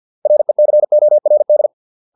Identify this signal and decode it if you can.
Morse code, heard as a keyed tone, about 50 wpm, 600 Hz. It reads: RECORD